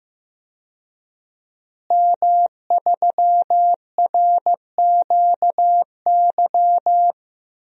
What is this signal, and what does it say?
Morse code, heard as a keyed tone, about 15 wpm, 700 Hz. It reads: M3RQY
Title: DXCC-100 Call Signs - 5 Characters - Repeated 2x - 15wpm